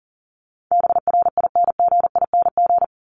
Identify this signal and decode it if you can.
Morse code, heard as a keyed tone, about 40 wpm, 700 Hz. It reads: BRINGING
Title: ING Suffix - Top 100 Words with Base Words 40wpm